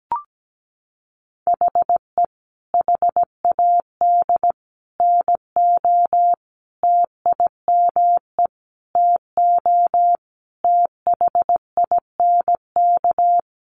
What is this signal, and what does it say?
Morse code, heard as a keyed tone, about 17 wpm, 700 Hz. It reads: HE HAD NO TIME TO THINK
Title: Sentences from Top 700 Words 17wpm